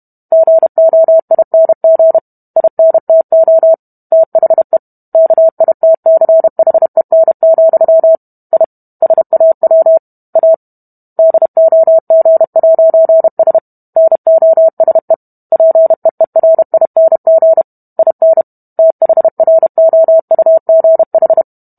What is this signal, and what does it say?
Morse code, heard as a keyed tone, about 32 wpm, 650 Hz. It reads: GOING INTO THE KITCHEN, I SAW A DOG'S NOSE PEERING IN THROUGH